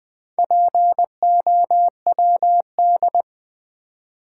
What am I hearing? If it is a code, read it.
Morse code, heard as a keyed tone, about 20 wpm, 700 Hz. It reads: POWD
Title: Build A Word - 2 Letters At A Time - 6-Letter Words - 20wpm